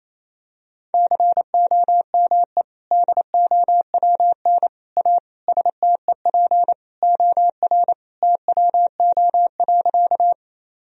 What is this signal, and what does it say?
Morse code, heard as a keyed tone, about 28 wpm, 700 Hz. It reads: COME DOWN A STEP OR TWO.
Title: Sentences from Top 1000 Words - Review Part 2 - 28wpm